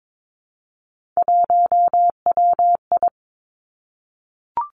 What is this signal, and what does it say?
Morse code, heard as a keyed tone, about 22 wpm, 700 Hz. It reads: 1WI
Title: Three Challenging Letter-Number Combinations 22wpm